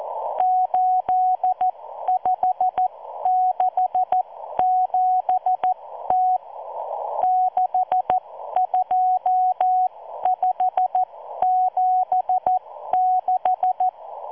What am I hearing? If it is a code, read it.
Morse code, heard as a keyed tone, about 14 wpm, spaced as for 12 wpm, 740 Hz. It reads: 8567T 62576